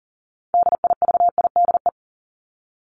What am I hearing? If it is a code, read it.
Morse code, heard as a keyed tone, about 40 wpm, 700 Hz. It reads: DIVIDE
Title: ED Suffix - Top 100 Words with Base Words 40wpm